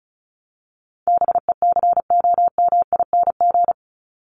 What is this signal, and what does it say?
Morse code, heard as a keyed tone, about 34 wpm, 700 Hz. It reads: BECOMING
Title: ING Suffix - Top 100 Words with Base Words 35wpm